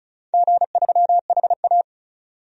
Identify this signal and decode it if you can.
Morse code, heard as a keyed tone, about 36 wpm, 700 Hz. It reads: G3HA
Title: DXCC-100 Call Signs - 4 Characters - 35wpm